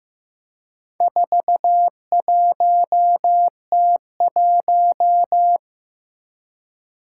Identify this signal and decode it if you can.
Morse code, heard as a keyed tone, about 15 wpm, 700 Hz. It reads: 41T1
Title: Four Challenging Letter-Number Combinations 15wpm